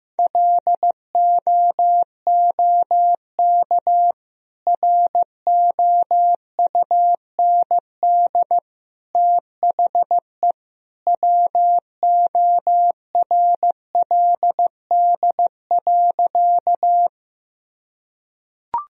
Morse code, heard as a keyed tone, about 14 wpm, 700 Hz. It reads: LOOK ROUND THE WORLD.
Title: Sentences from Top 1000 Words - Review Part 1 - 15wpm